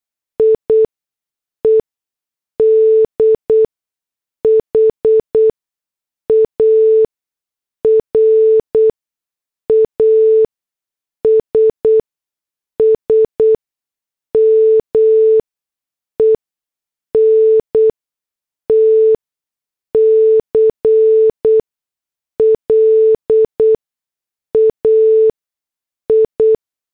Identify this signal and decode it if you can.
Morse code, heard as a keyed tone, about 8 wpm, spaced as for 6 wpm, 440 Hz. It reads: IED HARASSMENT CLAI